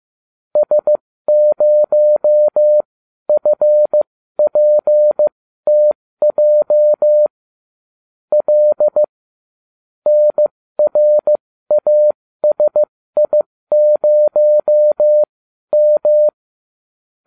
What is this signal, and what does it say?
Morse code, heard as a keyed tone, about 15 wpm, spaced as for 12 wpm, 600 Hz. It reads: S0FPTJ L NRASI0M